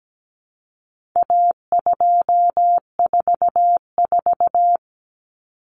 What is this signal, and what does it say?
Morse code, heard as a keyed tone, about 17 wpm, 700 Hz. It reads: A244
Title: Common Cut Numbers - 4-Digit 17wpm